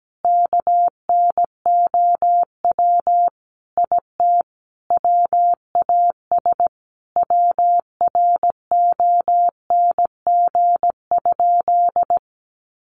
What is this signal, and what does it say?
Morse code, heard as a keyed tone, about 17 wpm, 700 Hz. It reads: KNOW IT WAS WRONG?
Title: Sentences from Top 1100 Words 17wpm